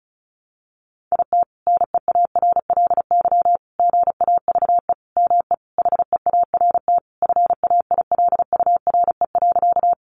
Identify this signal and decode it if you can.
Morse code, heard as a keyed tone, about 34 wpm, 700 Hz. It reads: IT NEARLY GAVE ME HEART FAILURE.
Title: Sentences from Top 1300 Words 35wpm